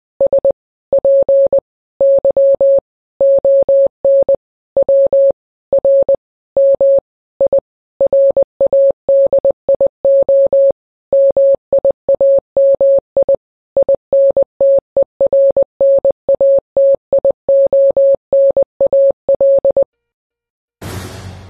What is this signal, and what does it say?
Morse code, heard as a keyed tone, about 20 wpm, 550 Hz. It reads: SPY ON WRMI RADIO MIAMI INTERNATIONAL
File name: radio / Imaginary Stations / KSPY 9395 kHz 29.1.23